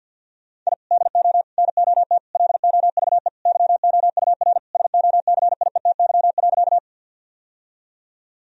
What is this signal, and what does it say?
Morse code, heard as a keyed tone, about 51 wpm, 700 Hz. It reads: I DO NOT LOVE YOUR SOCIETY.
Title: Sentences from Top 1000 Words - Review Part 4 - 50wpm